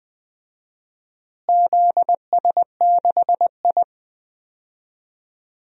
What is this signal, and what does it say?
Morse code, heard as a keyed tone, about 20 wpm, 700 Hz. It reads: ZS6I
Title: DXCC-100 Call Signs - 4 Characters - Repeated 3x - 20wpm